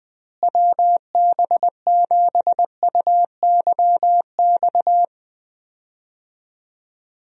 Morse code, heard as a keyed tone, about 20 wpm, 700 Hz. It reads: WB7UYX
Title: USA Call Signs - 6 Characters - Repeated 3x - 20wpm